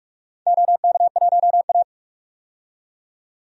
Morse code, heard as a keyed tone, about 46 wpm, 700 Hz. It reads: OK1A
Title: DXCC-100 Call Signs - 4 Characters - 45wpm